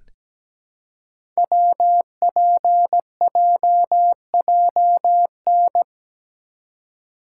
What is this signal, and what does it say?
Morse code, heard as a keyed tone, about 17 wpm, 700 Hz. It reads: WPJJN
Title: Code Groups - 5 Letters - 17wpm